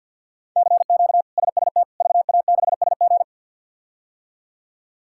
Morse code, heard as a keyed tone, about 52 wpm, 690 Hz. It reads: CQ SST VA6SG
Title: K1USN SST QSO Practice - 50wpm